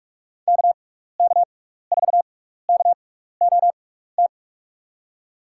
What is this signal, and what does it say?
Morse code, heard as a keyed tone, about 46 wpm, spaced as for 16 wpm, 700 Hz. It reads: KK4KOT
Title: USA Call Signs - 6 Characters - 6x Character Spacing - 45wpm